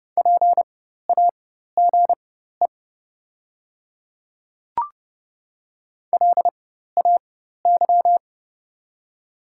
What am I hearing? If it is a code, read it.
Morse code, heard as a keyed tone, about 30 wpm, spaced as for 14 wpm, 690 Hz. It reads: PAGE LAY
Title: 4x Character Spacing - Top 500 Words - 30wpm